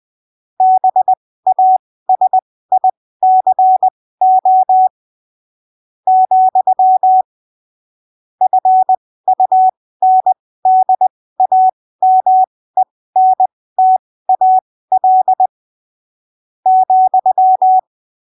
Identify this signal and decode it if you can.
Morse code, heard as a keyed tone, about 20 wpm, spaced as for 15 wpm, 750 Hz. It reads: BASICO , FUNDAMENTAL ,
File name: texto 09 20ppm